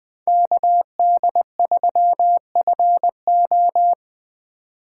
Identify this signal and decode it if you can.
Morse code, heard as a keyed tone, about 21 wpm, 700 Hz. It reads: KD3FO
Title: Speed Racing - USA 2x2 Call Signs 17wpm